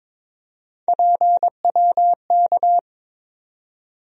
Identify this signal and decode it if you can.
Morse code, heard as a keyed tone, about 22 wpm, 700 Hz. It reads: PWK
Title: USA Call Sign Postfixes - 3 Letters 22wpm